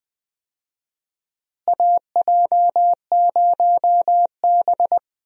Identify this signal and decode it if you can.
Morse code, heard as a keyed tone, about 21 wpm, 700 Hz. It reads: AJ0B
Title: USA Call Signs - 4 Characters - Repeated 3x - 20wpm